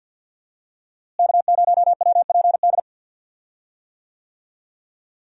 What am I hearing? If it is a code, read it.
Morse code, heard as a keyed tone, about 52 wpm, 700 Hz. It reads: K0WPD
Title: USA Call Signs - 5 Characters - Repeated 2x - 50wpm